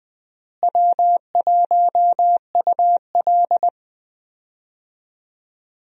Morse code, heard as a keyed tone, about 20 wpm, 700 Hz. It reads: W1UL
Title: USA Call Signs - 4 Characters - Repeated 2x - 20wpm